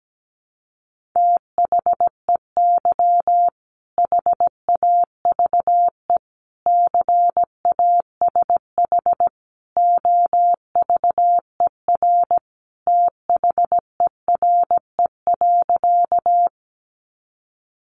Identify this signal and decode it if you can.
Morse code, heard as a keyed tone, about 17 wpm, 700 Hz. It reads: THEY HAVE CASH OVER THERE.
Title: Sentences from Top 700 Words 17wpm